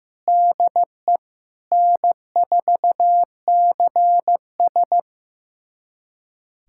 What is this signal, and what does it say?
Morse code, heard as a keyed tone, about 15 wpm, 700 Hz. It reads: DE N4CS
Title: USA Callsign DE Callsign - 4 Characters - 15wpm